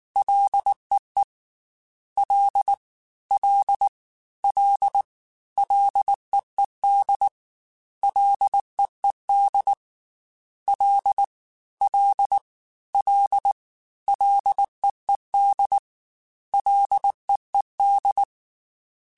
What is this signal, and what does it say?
Morse code, heard as a keyed tone, about 19 wpm, 800 Hz. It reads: LEE L L L LEED LEED L L L LEED LEED
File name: Ls / Single L m3s / Single L @18